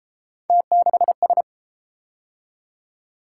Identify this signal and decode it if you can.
Morse code, heard as a keyed tone, about 31 wpm, 700 Hz. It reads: T6S